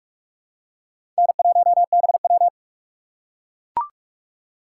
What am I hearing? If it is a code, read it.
Morse code, heard as a keyed tone, about 45 wpm, 700 Hz. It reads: N1BW
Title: USA Call Signs - 4 Characters - 3x Character Spacing - 45wpm